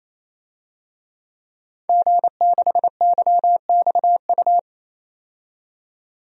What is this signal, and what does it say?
Morse code, heard as a keyed tone, about 27 wpm, 700 Hz. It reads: G6YXU
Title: DXCC-100 Call Signs - 5 Characters - 28wpm